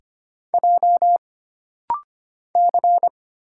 Morse code, heard as a keyed tone, about 25 wpm, 700 Hz. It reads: J C
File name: Single Character (Letter:Number:Punctuation:Prosign) - Rapid-Fire - 25wpm